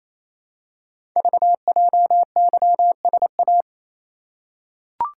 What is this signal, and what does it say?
Morse code, heard as a keyed tone, about 28 wpm, 700 Hz. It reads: VJYSA
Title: Code Groups - 5 Letters - 28wpm